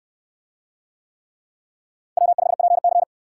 Morse code, heard as a keyed tone, about 68 wpm, 700 Hz. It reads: W5PC